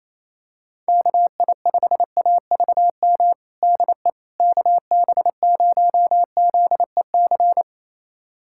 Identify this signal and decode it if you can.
Morse code, heard as a keyed tone, about 28 wpm, 700 Hz. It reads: KI5AVM DE KB0ZEC
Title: USA Callsign DE Callsign - 6 Characters - 28wpm